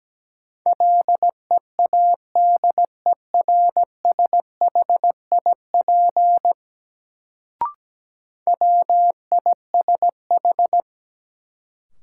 Morse code, heard as a keyed tone, about 17 wpm, 700 Hz. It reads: LEADERSHIP WISH
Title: Sentences from Top 1600 Words - New Word Drill - 17wpm